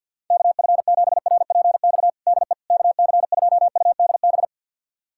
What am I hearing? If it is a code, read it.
Morse code, heard as a keyed tone, about 50 wpm, 700 Hz. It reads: KF7RPX DE KC2UDB
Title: USA Callsign DE Callsign - 6 Characters - 50wpm